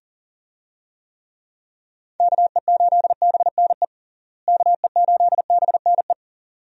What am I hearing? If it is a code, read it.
Morse code, heard as a keyed tone, about 40 wpm, 700 Hz. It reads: KE8BNE KE8BNE
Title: USA Call Signs - 6 Characters - Repeated 2x - 40wpm